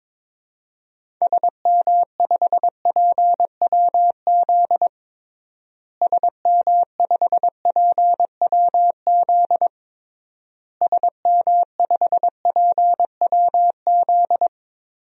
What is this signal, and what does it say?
Morse code, heard as a keyed tone, about 22 wpm, 700 Hz. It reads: SM5PWZ SM5PWZ SM5PWZ
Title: DXCC-100 Call Signs - 6 Characters - Repeated 3x - 22wpm